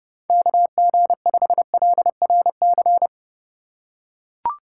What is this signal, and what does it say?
Morse code, heard as a keyed tone, about 32 wpm, 700 Hz. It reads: KG5LRC
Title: USA Call Signs - 6 Characters - 3x Character Spacing - 30wpm